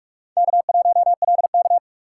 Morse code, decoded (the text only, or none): K1LK